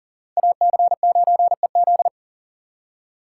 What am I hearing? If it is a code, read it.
Morse code, heard as a keyed tone, about 52 wpm, 700 Hz. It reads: AC9EZ